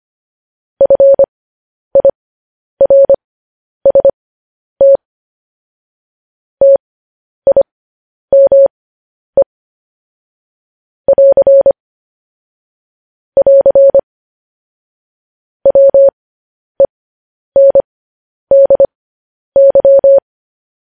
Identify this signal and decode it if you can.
Morse code, heard as a keyed tone, about 25 wpm, spaced as for 10 wpm, 550 Hz. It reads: FIRST TIME <AR> <AR> WENDY